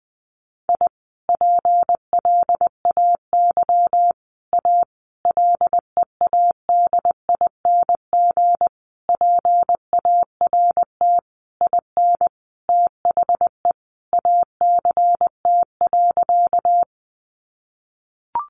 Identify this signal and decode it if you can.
Morse code, heard as a keyed tone, about 19 wpm, 700 Hz. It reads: I PLAY A LEADING PART IN THE ACT.
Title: Sentences from Top 1100 Words 20wpm